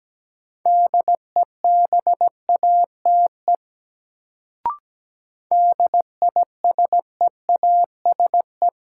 Morse code, heard as a keyed tone, about 17 wpm, 700 Hz. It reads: DEBATE DISEASE
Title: Sentences from Top 1100 Words - New Word Drill Alt - 17wpm